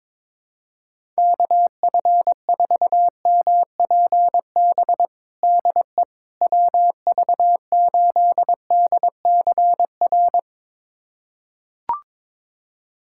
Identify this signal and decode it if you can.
Morse code, heard as a keyed tone, about 22 wpm, 700 Hz. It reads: KF4MPB DE WV8DCR
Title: USA Callsign DE Callsign 22wpm